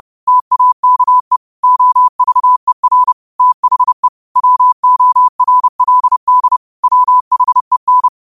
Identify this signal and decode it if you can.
Morse code, heard as a keyed tone, about 30 wpm, 1000 Hz. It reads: TAKE OVER THE WORLD WHEN
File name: qotdfiles / 30wpm